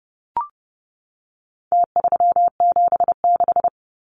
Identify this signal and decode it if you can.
Morse code, heard as a keyed tone, about 30 wpm, 700 Hz. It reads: T376
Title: Numbers - Contest Style - 4-Digit 30wpm